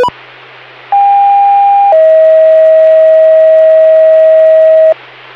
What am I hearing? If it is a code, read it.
Two-tone page, heard as a one-second tone, then a three-second tone, about 800 then 640 Hz. It is heard from 0.9 to 4.9 s.